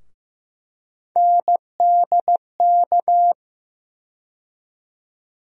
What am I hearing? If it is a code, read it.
Morse code, heard as a keyed tone, about 15 wpm, 700 Hz. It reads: NDK
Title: USA Call Sign Postfixes - 3 Letters 15wpm